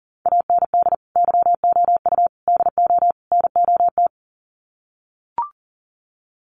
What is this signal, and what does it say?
Morse code, heard as a keyed tone, about 40 wpm, 700 Hz. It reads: AND YOU DO NOT
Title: Sets of 4 Words 40wpm